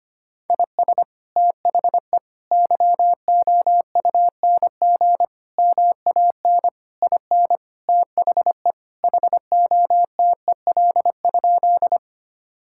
Morse code, heard as a keyed tone, about 25 wpm, 700 Hz. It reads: IS THE YOUNG MAN IN THE HOTEL?
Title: Sentences from Top 1600 Words 25wpm